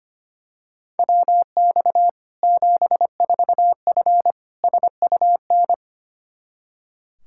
Morse code, heard as a keyed tone, about 25 wpm, 700 Hz. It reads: WX 74F SUN
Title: Practice QSO - Exchange 2 - Info - 25wpm